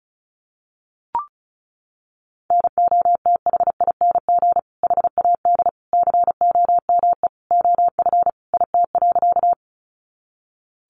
Morse code, heard as a keyed tone, about 34 wpm, 700 Hz. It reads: NOTHING HAD COME OF IT.
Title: Sentences from Top 500 Words 35wpm